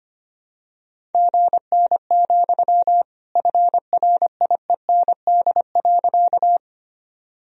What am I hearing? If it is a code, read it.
Morse code, heard as a keyed tone, about 25 wpm, 700 Hz. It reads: GN, FRIEND.
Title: Sentences from Top 300 Words and Abbreviations 25wpm